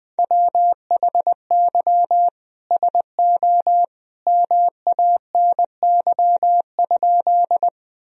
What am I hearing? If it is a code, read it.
Morse code, heard as a keyed tone, about 20 wpm, 700 Hz. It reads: WHY SO MANY?
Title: Sentences from Top 1000 Words - Review Part 1 - 20wpm